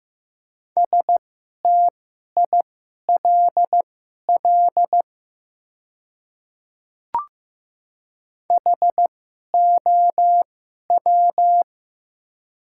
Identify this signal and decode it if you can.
Morse code, heard as a keyed tone, about 15 wpm, spaced as for 11 wpm, 700 Hz. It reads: STILL HOW
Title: 2x Character Spacing - Top 500 Words - 15wpm